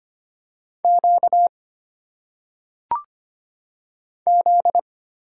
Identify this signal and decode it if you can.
Morse code, heard as a keyed tone, about 25 wpm, 700 Hz. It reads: Q Z
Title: Single Letters - Phonetic Pronunciation 25wpm